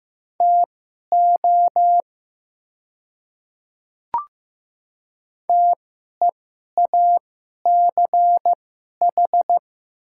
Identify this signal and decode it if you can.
Morse code, heard as a keyed tone, about 15 wpm, spaced as for 11 wpm, 690 Hz. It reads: TO TEACH
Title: 2x Character Spacing - Top 500 Words - 15wpm